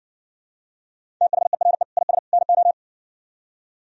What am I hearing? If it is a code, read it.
Morse code, heard as a keyed tone, about 61 wpm, 700 Hz. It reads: THERE IS NO